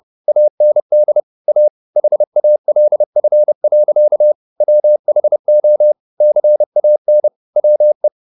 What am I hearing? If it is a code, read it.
Morse code, heard as a keyed tone, about 30 wpm, 600 Hz. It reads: AND A HALF. WHO CAN WE
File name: the old man and the sea / the old man and the sea 1 30wpm